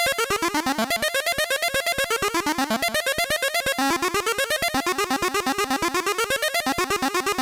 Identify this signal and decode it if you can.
There is no code.